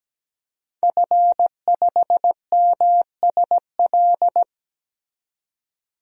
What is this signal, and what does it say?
Morse code, heard as a keyed tone, about 17 wpm, 700 Hz. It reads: F5MSL
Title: DXCC-100 Call Signs - 5 Characters - 17wpm